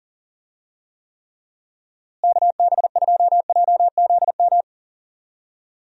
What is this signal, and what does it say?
Morse code, heard as a keyed tone, about 40 wpm, 700 Hz. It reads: KB2JZM
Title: USA Call Signs - 6 Characters - Repeated 3x - 40wpm